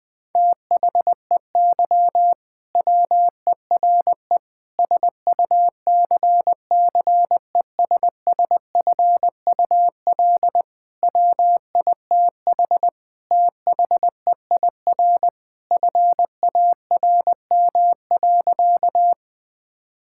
Morse code, heard as a keyed tone, about 20 wpm, 700 Hz. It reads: THEY WERE SUCCESSFUL WITH THEIR FARM.
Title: Sentences from Top 1000 Words - Review Part 6 - 20wpm